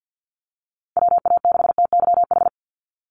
Morse code, heard as a keyed tone, about 50 wpm, 700 Hz. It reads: WA6TYH
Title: USA Call Signs 50wpm